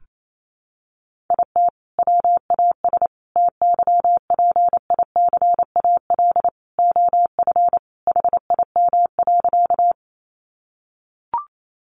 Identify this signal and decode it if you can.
Morse code, heard as a keyed tone, about 28 wpm, 700 Hz. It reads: IT WAS TYPICAL OF HIM.
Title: Sentences from Top 1100 Words 28wpm